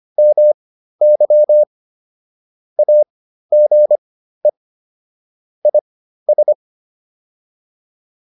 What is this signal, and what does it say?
Morse code, heard as a keyed tone, about 25 wpm, spaced as for 13 wpm, 600 Hz. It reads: MY AGE IS